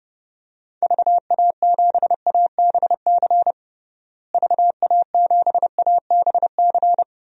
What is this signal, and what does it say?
Morse code, heard as a keyed tone, about 30 wpm, 700 Hz. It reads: VA7ABC VA7ABC
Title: DXCC-100 Call Signs - 6 Characters - Repeated 2x - 30wpm